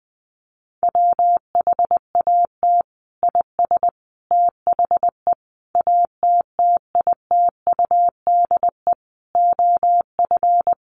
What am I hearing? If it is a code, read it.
Morse code, heard as a keyed tone, about 20 wpm, 700 Hz. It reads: WHAT IS THE ATTITUDE OF
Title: Sentences from Top 1300 Words 20wpm